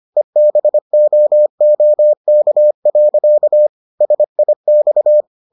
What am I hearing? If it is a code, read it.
Morse code, heard as a keyed tone, about 25 wpm, 600 Hz. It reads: EBOOK. SIX